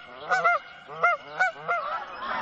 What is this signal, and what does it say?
There is no code.